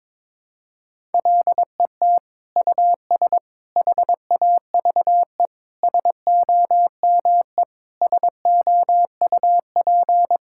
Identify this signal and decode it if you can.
Morse code, heard as a keyed tone, about 22 wpm, 700 Hz. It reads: LET US HAVE SOME SOUP
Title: Sentences from Top 1000 Words - Review Part 6 - 22wpm